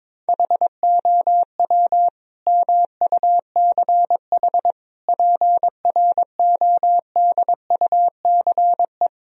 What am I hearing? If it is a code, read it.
Morse code, heard as a keyed tone, about 22 wpm, 700 Hz. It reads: HOW MUCH PRODUCE
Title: Sentences from Top 1000 Words - Review Part 3 - 22wpm